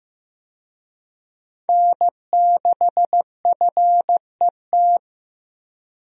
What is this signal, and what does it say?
Morse code, heard as a keyed tone, about 15 wpm, 700 Hz. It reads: N6FET
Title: USA Call Signs - 5 Characters - Repeated 2x - 15wpm